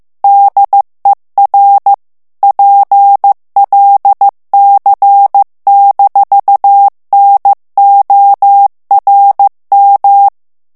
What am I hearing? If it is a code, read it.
Morse code, heard as a keyed tone, about 15 wpm, 800 Hz. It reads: DER PLC-NORM
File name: ALLEMAND texte clair 1 vitesse 14 mots minute